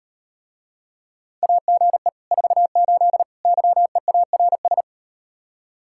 Morse code, heard as a keyed tone, about 38 wpm, 700 Hz. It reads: AGE 48 YEARS